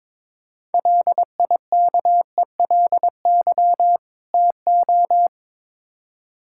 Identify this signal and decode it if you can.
Morse code, heard as a keyed tone, about 21 wpm, 700 Hz. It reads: LIKELY TO
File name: Build A Phrase - 2-Word Phrases - 22wpm